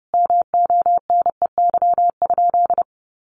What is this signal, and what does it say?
Morse code, heard as a keyed tone, about 30 wpm, 700 Hz. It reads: MONEY?
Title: Sentences from Top 1400 Words 30wpm